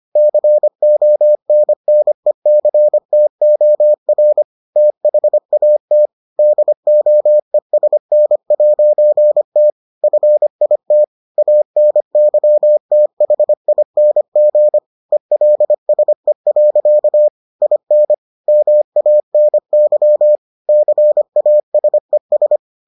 Morse code, heard as a keyed tone, about 25 wpm, 600 Hz. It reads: CONNECTOR THAT DOESN'T FIT ANYTHING ELSE. IN MANY CASES